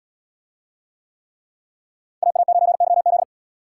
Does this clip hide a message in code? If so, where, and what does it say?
Yes: Morse code from 2.2 to 3.2 s, heard as a keyed tone, about 66 wpm, 700 Hz. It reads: AA9YZ